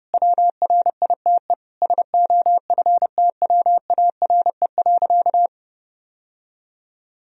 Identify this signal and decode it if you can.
Morse code, heard as a keyed tone, about 30 wpm, 700 Hz. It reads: WRITE SOFTWARE.
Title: Sentences from Top 1000 Words - Review Part 3 - 30wpm